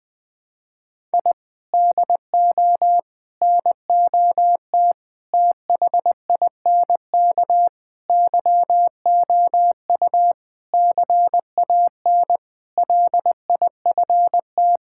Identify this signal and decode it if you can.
Morse code, heard as a keyed tone, about 20 wpm, 700 Hz. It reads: I DO NOT THINK YOU CAN LIFT
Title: Sentences with 5-Word Phrases - Spoken Beginning - 20wpm